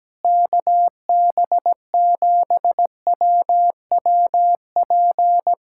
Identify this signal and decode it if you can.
Morse code, heard as a keyed tone, about 17 wpm, 700 Hz. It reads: KB7WWP
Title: Speed-Racing Common QSO Phrases Encore - 17wpm